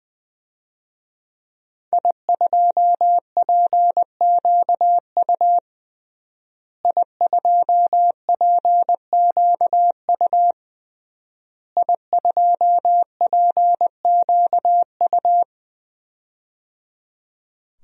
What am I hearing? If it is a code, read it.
Morse code, heard as a keyed tone, about 20 wpm, 700 Hz. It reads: I2PQU I2PQU I2PQU
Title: DXCC-100 Call Signs - 5 Characters - Repeated 3x - 20wpm